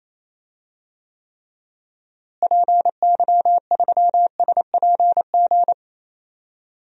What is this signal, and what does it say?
Morse code, heard as a keyed tone, about 28 wpm, 700 Hz. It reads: PY3SPG
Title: DXCC-100 Call Signs - 6 Characters - Repeated 2x - 28wpm